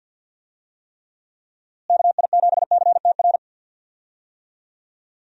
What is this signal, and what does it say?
Morse code, heard as a keyed tone, about 51 wpm, 700 Hz. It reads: KI7CTR
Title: USA Call Signs - 6 Characters - Repeated 2x - 50wpm